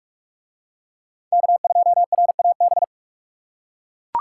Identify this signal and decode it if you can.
Morse code, heard as a keyed tone, about 41 wpm, 700 Hz. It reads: K2RAB